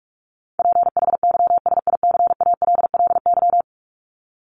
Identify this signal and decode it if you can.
Morse code, heard as a keyed tone, about 44 wpm, 700 Hz. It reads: PHYSICALLY
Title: Top 1501-1750 Words 45wpm